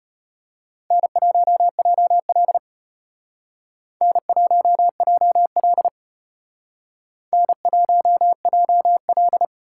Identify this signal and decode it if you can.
Morse code, heard as a keyed tone, about 32 wpm, 700 Hz. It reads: N1JL N1JL N1JL